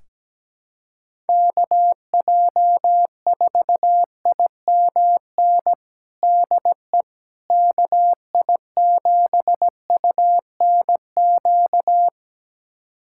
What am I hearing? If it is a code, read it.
Morse code, heard as a keyed tone, about 17 wpm, 700 Hz. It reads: KJ4IMN DE KI7UNQ
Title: USA Callsign DE Callsign - 6 Characters - 17wpm